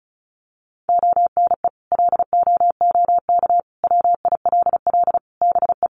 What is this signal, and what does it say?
Morse code, heard as a keyed tone, about 34 wpm, 700 Hz. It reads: ONE LOOK WILL BE
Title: Sentences from Top 1400 Words 35wpm